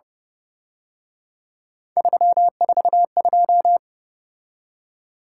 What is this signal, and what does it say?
Morse code, heard as a keyed tone, about 29 wpm, 700 Hz. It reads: 342